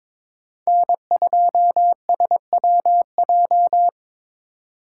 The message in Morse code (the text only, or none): N2SWJ